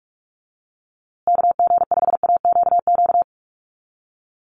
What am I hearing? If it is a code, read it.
Morse code, heard as a keyed tone, about 44 wpm, 690 Hz. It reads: KG5AQQ
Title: USA Call Signs Repeated 3x 45wpm